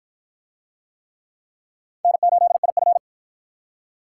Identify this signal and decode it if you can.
Morse code, heard as a keyed tone, about 53 wpm, 700 Hz. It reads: N8IF